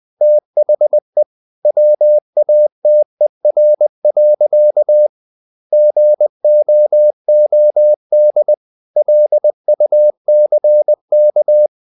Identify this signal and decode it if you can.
Morse code, heard as a keyed tone, about 20 wpm, 600 Hz. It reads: THE WATER. GOOD LUCK
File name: the old man and the sea / the old man and the sea part 1 20wpm